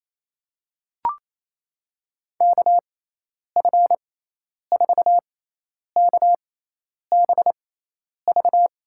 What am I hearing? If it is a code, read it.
Morse code, heard as a keyed tone, about 28 wpm, spaced as for 10 wpm, 700 Hz. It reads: KF4KBV
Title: USA Call Signs - 6x Character Spacing - 28wpm